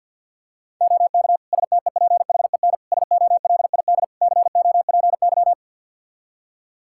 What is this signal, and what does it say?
Morse code, heard as a keyed tone, about 47 wpm, 700 Hz. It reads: OK STEPHEN SOLID COPY
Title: K1USN SST QSO Practice 50wpm